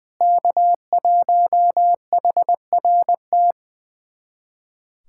Morse code, heard as a keyed tone, about 20 wpm, 700 Hz. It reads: K1HRT
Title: USA Callsign DE Callsign - 5 Characters - 20wpm